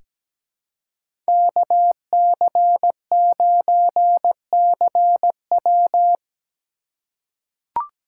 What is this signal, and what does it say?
Morse code, heard as a keyed tone, about 17 wpm, 700 Hz. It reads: KC9CW
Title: USA Call Signs - 5 Characters - 4x Character Spacing - 17wpm